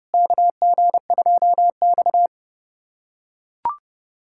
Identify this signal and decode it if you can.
Morse code, heard as a keyed tone, about 30 wpm, 700 Hz. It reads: KG2X
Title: USA Callsign DE Callsign - 4 Characters - 30wpm